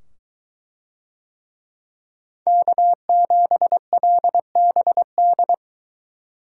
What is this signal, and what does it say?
Morse code, heard as a keyed tone, about 21 wpm, 700 Hz. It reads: K7LBD